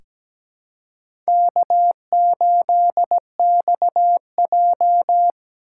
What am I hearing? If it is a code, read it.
Morse code, heard as a keyed tone, about 17 wpm, 700 Hz. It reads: K8XJ
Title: K1USN SST QSO Practice - 17wpm